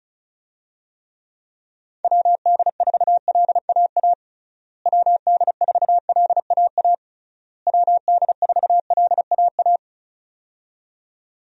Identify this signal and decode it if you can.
Morse code, heard as a keyed tone, about 34 wpm, 700 Hz. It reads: WD4LAA WD4LAA WD4LAA
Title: USA Call Signs - 6 Characters - Repeated 3x - 35wpm